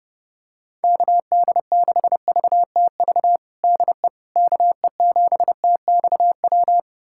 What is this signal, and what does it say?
Morse code, heard as a keyed tone, about 30 wpm, 700 Hz. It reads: KD6VTV DE KE7TXW
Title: USA Callsign DE Callsign 30wpm